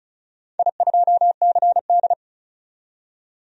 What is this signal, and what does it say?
Morse code, heard as a keyed tone, about 35 wpm, 700 Hz. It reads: I2CD
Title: DXCC-100 Call Signs - 4 Characters - 35wpm